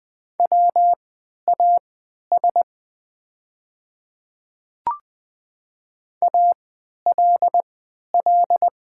Morse code, heard as a keyed tone, about 20 wpm, spaced as for 11 wpm, 700 Hz. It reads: WAS ALL
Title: 3x Character Spacing - Top 300 Words - 20wpm